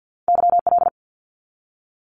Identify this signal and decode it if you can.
Morse code, heard as a keyed tone, about 50 wpm, 700 Hz. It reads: YL
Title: Abbreviations 50wpm